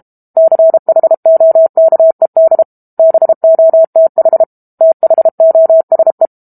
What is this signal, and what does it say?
Morse code, heard as a keyed tone, about 32 wpm, 650 Hz. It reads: CHOKED BOTH THOSE
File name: Book Two / 06-Chapter 6 - 32 WPM